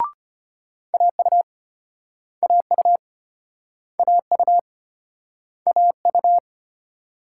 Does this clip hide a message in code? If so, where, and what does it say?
Yes: Morse code from 0.9 to 6.4 s, heard as a keyed tone, about 33 wpm, 700 Hz. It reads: AU AU AU AU